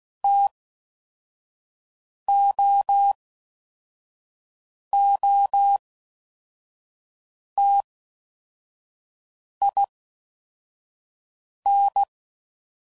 Morse code, heard as a keyed tone, about 16 wpm, 800 Hz. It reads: T O O T I N
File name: Mixed / Variable Sp / T AOEIs